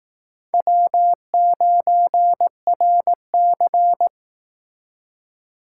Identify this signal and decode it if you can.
Morse code, heard as a keyed tone, about 22 wpm, 700 Hz. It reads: W9RC